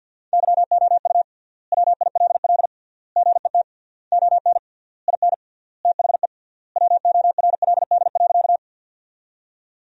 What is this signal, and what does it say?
Morse code, heard as a keyed tone, about 53 wpm, 700 Hz. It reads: YOU WILL GET ON IN THE WORLD.
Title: 3x Word-Spacing - Sentences from Top 200 Words - 50wpm